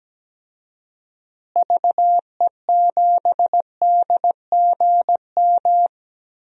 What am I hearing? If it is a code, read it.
Morse code, heard as a keyed tone, about 17 wpm, 700 Hz. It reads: VE7DGM
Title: DXCC-100 Call Signs - 6 Characters - Repeated 2x - 17wpm